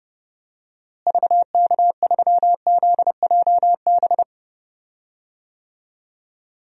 Morse code, heard as a keyed tone, about 30 wpm, 700 Hz. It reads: VK3ZJB
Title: DXCC-100 Call Signs - 6 Characters - Repeated 2x - 30wpm